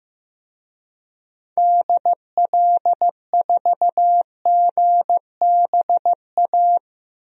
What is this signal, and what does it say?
Morse code, heard as a keyed tone, about 15 wpm, 700 Hz. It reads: DL4GBA
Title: DXCC-100 Call Signs - 6 Characters - Repeated 3x - 15wpm